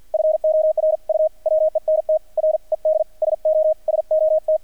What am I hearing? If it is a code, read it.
Morse code, heard as a keyed tone, about 48 wpm, 630 Hz. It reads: FOR U WENT RED SO SOT
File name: archived / CRSnet